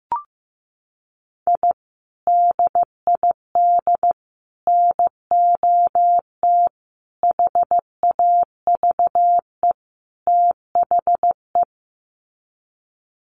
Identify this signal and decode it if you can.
Morse code, heard as a keyed tone, about 15 wpm, 700 Hz. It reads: I DID NOT HAVE THE
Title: Sets of 5 Words Encore 15wpm